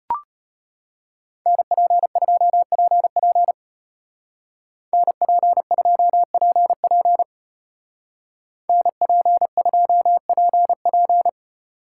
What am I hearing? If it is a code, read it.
Morse code, heard as a keyed tone, about 30 wpm, 700 Hz. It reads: NP2PP NP2PP NP2PP
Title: Speed Racing - USA 2x2 Call Signs 25wpm